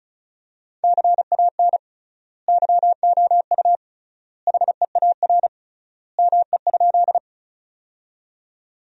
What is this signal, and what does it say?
Morse code, heard as a keyed tone, about 36 wpm, 700 Hz. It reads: CAN YOU HEAR ME?
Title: 3x Word-Spacing - Sentences from Top 300 Words - 35wpm